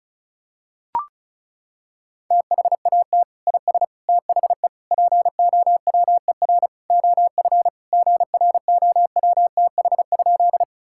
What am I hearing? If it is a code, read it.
Morse code, heard as a keyed tone, about 36 wpm, 700 Hz. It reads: THAT IS THE POWER OF GROWTH?
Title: Sentences from Top 1000 Words - Review Part 5 - 35wpm